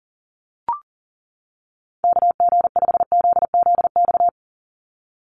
Morse code, heard as a keyed tone, about 40 wpm, 700 Hz. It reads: KG5ZZX
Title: USA Call Signs 40wpm